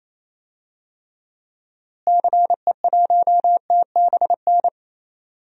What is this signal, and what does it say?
Morse code, heard as a keyed tone, about 28 wpm, 700 Hz. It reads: CE1TBN
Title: DXCC-100 Call Signs - 6 Characters - Repeated 3x - 28wpm